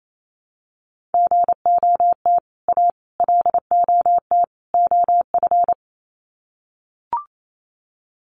Morse code, heard as a keyed tone, about 28 wpm, 700 Hz. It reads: GOT A LOT OF
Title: Sets of 4 Words 28wpm